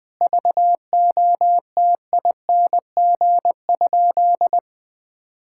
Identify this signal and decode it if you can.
Morse code, heard as a keyed tone, about 20 wpm, 700 Hz. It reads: VOTING?
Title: Sentences with 9-Word Phrases - 20wpm